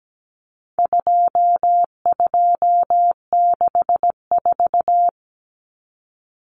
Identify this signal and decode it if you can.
Morse code, heard as a keyed tone, about 17 wpm, 700 Hz. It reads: 2264
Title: Numbers - 4-Digit 17wpm